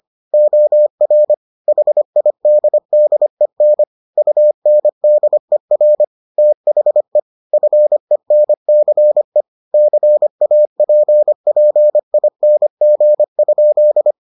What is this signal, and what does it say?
Morse code, heard as a keyed tone, about 25 wpm, 600 Hz. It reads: OR HIDDEN UNDER THE FENCE CAPPING?